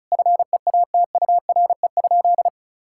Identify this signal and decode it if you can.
Morse code, heard as a keyed tone, about 36 wpm, 700 Hz. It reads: FEATURE?
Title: Sentences from Top 1000 Words - Review Part 5 - 35wpm